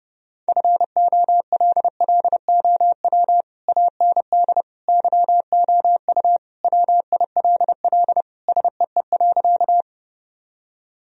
Morse code, heard as a keyed tone, about 30 wpm, 700 Hz. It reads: FOLLOW AND YOU WILL SEE.
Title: Sentences from Top 1000 Words - Review Part 1 - 30wpm